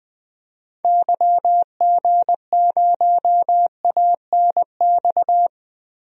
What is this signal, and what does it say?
Morse code, heard as a keyed tone, about 21 wpm, 700 Hz. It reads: YG0ANX
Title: DXCC-100 Call Signs - 6 Characters - 20wpm